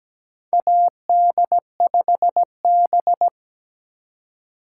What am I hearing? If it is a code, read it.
Morse code, heard as a keyed tone, about 17 wpm, 700 Hz. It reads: AD5B
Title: USA Call Signs - 4 Characters - 2x Character Spacing - 17wpm